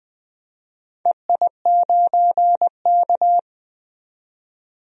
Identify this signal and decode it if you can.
Morse code, heard as a keyed tone, about 20 wpm, 700 Hz. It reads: EI9K
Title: DXCC-100 Call Signs - 4 Characters - 20wpm